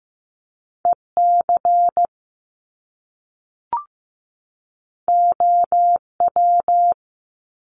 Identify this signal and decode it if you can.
Morse code, heard as a keyed tone, about 15 wpm, 700 Hz. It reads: EC OW
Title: Top Two Letter Combinations in Words 15wpm